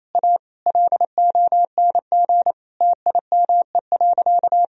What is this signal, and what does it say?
Morse code, heard as a keyed tone, about 28 wpm, 700 Hz. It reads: A LONG TIME.
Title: Sentences from Top 1000 Words - Review Part 4 - 28wpm